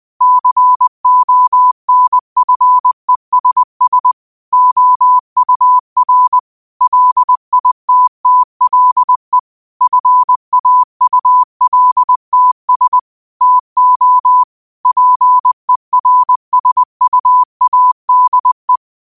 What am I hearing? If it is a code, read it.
Morse code, heard as a keyed tone, about 20 wpm, 1000 Hz. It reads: CONFESS OUR LITTLE FAULTS TO PERSUADE
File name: 20wpm